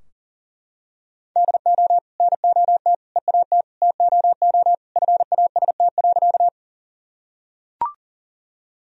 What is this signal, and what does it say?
Morse code, heard as a keyed tone, about 40 wpm, 700 Hz. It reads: DO NOT EAT TOO FAST.
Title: Sentences from Top 1000 Words - Review Part 2 - 40wpm